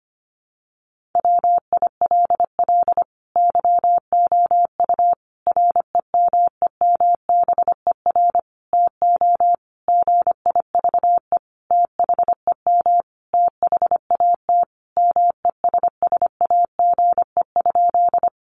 Morse code, heard as a keyed tone, about 25 wpm, 700 Hz. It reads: WILL YOU REMEMBER TO GIVE THEM THAT MESSAGE?
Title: Sentences from Top 1100 Words 25wpm